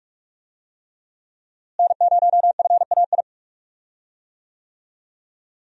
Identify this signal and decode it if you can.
Morse code, heard as a keyed tone, about 41 wpm, 700 Hz. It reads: N0FAI
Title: USA Call Signs - 5 Characters - Repeated 3x - 45wpm